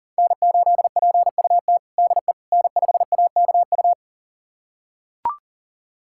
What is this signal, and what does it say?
Morse code, heard as a keyed tone, about 40 wpm, 700 Hz. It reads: N8PUT DE N5AKU
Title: USA Callsign DE Callsign - 5 Characters - 40wpm